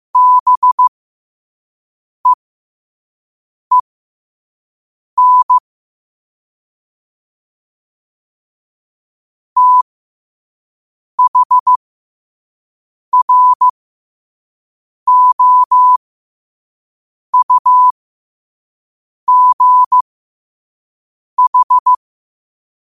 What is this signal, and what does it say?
Morse code, heard as a keyed tone, about 15 wpm, spaced as for 5 wpm, 1000 Hz. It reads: BEEN THROUGH
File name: qotdfiles / 5wpm